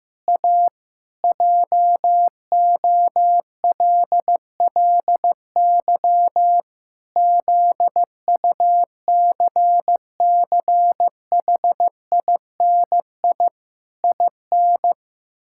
Morse code, heard as a keyed tone, about 15 wpm, 700 Hz. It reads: A JOLLY ZUCCHINI IN
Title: Pangram Challenge v2 - 15wpm